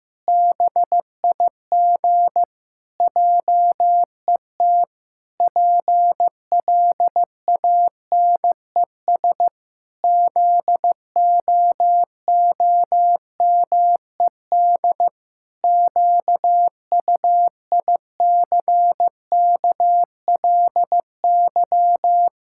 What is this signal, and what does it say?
Morse code, heard as a keyed tone, about 15 wpm, 700 Hz. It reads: BIG JET PLANES ZOOMED QUICKLY
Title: Pangram Challenge v2 - 15wpm